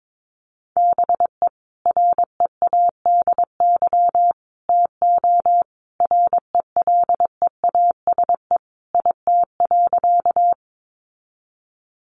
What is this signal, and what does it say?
Morse code, heard as a keyed tone, about 22 wpm, 700 Hz. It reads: BE READY TO RELEASE IT.
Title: Sentences from Top 1100 Words 22wpm